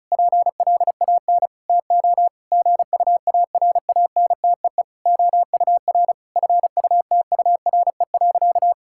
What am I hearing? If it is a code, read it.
Morse code, heard as a keyed tone, about 36 wpm, 700 Hz. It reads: PLAN TO GUARANTEE OUR FUTURE.
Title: Sentences from Top 1600 Words 35wpm